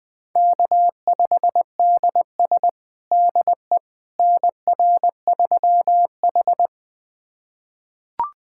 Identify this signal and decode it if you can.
Morse code, heard as a keyed tone, about 20 wpm, 700 Hz. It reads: K5DS DE NR3H
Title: USA Callsign DE Callsign - 4 Characters - 20wpm